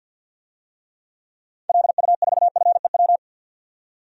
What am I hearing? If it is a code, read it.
Morse code, heard as a keyed tone, about 47 wpm, 700 Hz. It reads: LU4FEW
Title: DXCC-100 Call Signs - 6 Characters - 50wpm